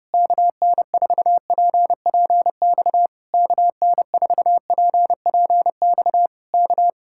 Morse code, heard as a keyed tone, about 28 wpm, 700 Hz. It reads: KN4PPX KN4PPX K
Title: Speed-Racing Common QSO Phrases Encore - 30wpm